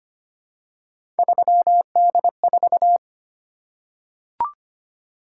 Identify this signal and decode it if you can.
Morse code, heard as a keyed tone, about 25 wpm, 700 Hz. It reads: 3D4
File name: Three Letter-Number Combinations - 25wpm